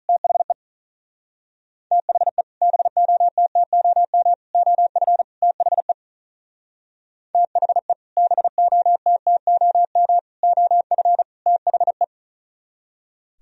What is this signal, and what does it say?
Morse code, heard as a keyed tone, about 45 wpm, 700 Hz. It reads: THE THE BOTTOM OF THE THE BOTTOM OF THE